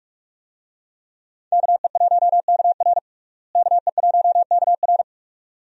Morse code, heard as a keyed tone, about 42 wpm, 700 Hz. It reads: KE1KR KE1KR
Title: USA Call Signs - 5 Characters - Repeated 2x - 45wpm